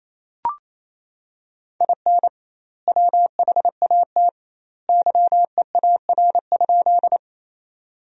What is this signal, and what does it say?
Morse code, heard as a keyed tone, about 28 wpm, 700 Hz. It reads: IN WHAT YEAR?
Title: 2x Word-Spacing - Sentences from Top 200 Words - 28wpm